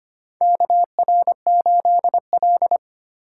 Morse code, heard as a keyed tone, about 25 wpm, 700 Hz. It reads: KR8L
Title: K1USN SST QSO Practice - 25wpm